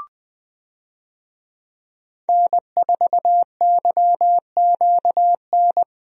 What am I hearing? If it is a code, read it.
Morse code, heard as a keyed tone, about 21 wpm, 700 Hz. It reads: N4YQN
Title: USA Call Signs - 5 Characters - Repeated 3x - 20wpm